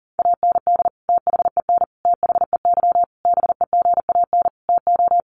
Morse code, heard as a keyed tone, about 37 wpm, 700 Hz. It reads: AND THEN THEY BEGAN TO
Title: Sentences from Top 300 Words 40wpm